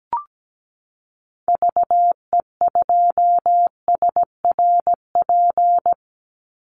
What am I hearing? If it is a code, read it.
Morse code, heard as a keyed tone, about 17 wpm, 700 Hz. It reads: VE2SRP
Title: Call Signs - 17wpm